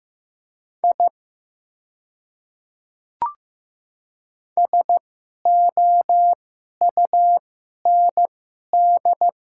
Morse code, heard as a keyed tone, about 15 wpm, spaced as for 11 wpm, 700 Hz. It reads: I SOUND
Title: 2x Character Spacing - Top 100 Words - 15wpm